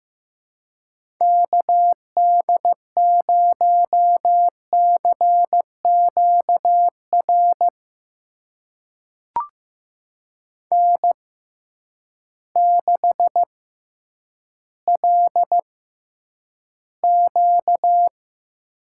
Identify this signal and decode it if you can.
Morse code, heard as a keyed tone, about 15 wpm, spaced as for 9 wpm, 700 Hz. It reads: KD0CQR N6LQ
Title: USA Call Signs - 6x Character Spacing - 15wpm